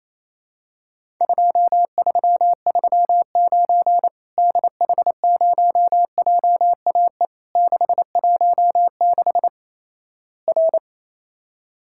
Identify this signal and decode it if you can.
Morse code, heard as a keyed tone, about 28 wpm, 700 Hz. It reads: 2339 DH0JAE 616 R
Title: Worked All Europe DX Contest - QTC Exchange - 28wpm